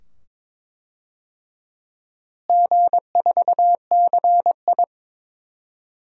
Morse code, heard as a keyed tone, about 22 wpm, 700 Hz. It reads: G4CI
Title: DXCC-100 Call Signs - 4 Characters - Repeated 2x - 22wpm